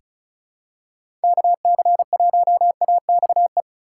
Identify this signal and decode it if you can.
Morse code, heard as a keyed tone, about 36 wpm, 700 Hz. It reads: KC1AXE
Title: USA Call Signs - 6 Characters - 4x Character Spacing - 35wpm